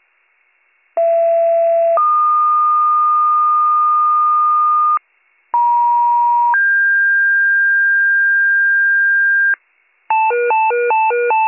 Two-tone page, heard as a one-second tone, then a three-second tone, about 670 then 1220 Hz, about 950 then 1600 Hz, with noise. It is heard from 1.0 to 9.5 s.